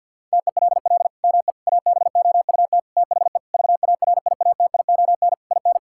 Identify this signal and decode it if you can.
Morse code, heard as a keyed tone, about 50 wpm, 700 Hz. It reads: TELL ME ABOUT THE VARIATION IN
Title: Sentences from Top 1600 Words 50wpm